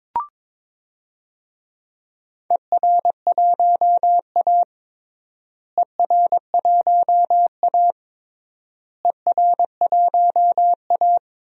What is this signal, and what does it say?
Morse code, heard as a keyed tone, about 22 wpm, 700 Hz. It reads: ER1A ER1A ER1A
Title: DXCC-100 Call Signs - 4 Characters - Repeated 3x - 22wpm